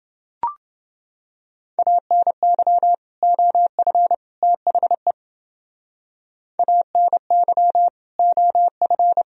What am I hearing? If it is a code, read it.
Morse code, heard as a keyed tone, about 28 wpm, 700 Hz. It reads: ANY OF THE ANY OF